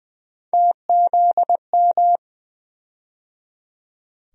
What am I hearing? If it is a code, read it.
Morse code, heard as a keyed tone, about 20 wpm, 700 Hz. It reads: TZM
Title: USA Call Sign Postfixes - 3 Letters 20wpm